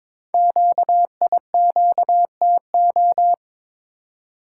Morse code, heard as a keyed tone, about 22 wpm, 700 Hz. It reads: QIQTO
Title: Code Groups - 5 Letters - 22wpm